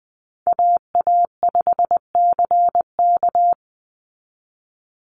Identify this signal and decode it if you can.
Morse code, heard as a keyed tone, about 20 wpm, 700 Hz. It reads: AA5CK
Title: NA SOTA Chaser Callsigns 20wpm